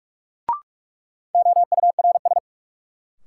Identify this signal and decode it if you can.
Morse code, heard as a keyed tone, about 43 wpm, 700 Hz. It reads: OURS
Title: Sentences from Top 1300 Words - New Word Drill Alt - 45wpm